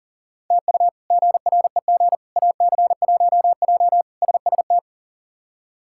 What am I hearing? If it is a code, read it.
Morse code, heard as a keyed tone, about 40 wpm, 700 Hz. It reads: TU GREG AC1J SST
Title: K1USN SST QSO Practice - 40wpm